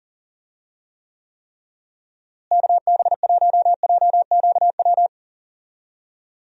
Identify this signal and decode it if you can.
Morse code, heard as a keyed tone, about 40 wpm, 700 Hz. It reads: KB1JQW
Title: USA Call Signs - 6 Characters - Repeated 3x - 40wpm